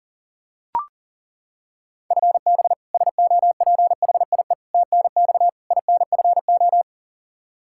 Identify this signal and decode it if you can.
Morse code, heard as a keyed tone, about 40 wpm, 700 Hz. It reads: FB SOPHIE TNX INFO
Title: K1USN SST QSO Practice 40wpm